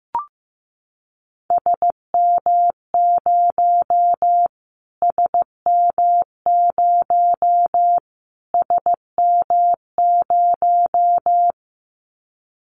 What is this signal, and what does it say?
Morse code, heard as a keyed tone, about 15 wpm, 700 Hz. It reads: SM0 SM0 SM0
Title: Callsign Prefixes for Top 25 DXCC - Repeated 3x - 15wpm